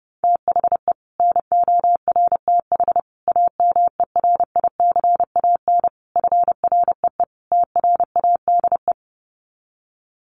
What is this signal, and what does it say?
Morse code, heard as a keyed tone, about 30 wpm, 700 Hz. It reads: THE NORTH AMERICAN FREE TRADE
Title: Sets of 5 Words Encore 30wpm